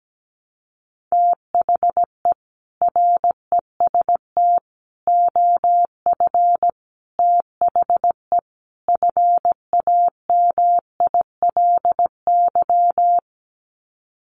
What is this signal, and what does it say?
Morse code, heard as a keyed tone, about 17 wpm, 700 Hz. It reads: THE REST OF THE FAMILY
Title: Sets of 5 Words Encore2 17wpm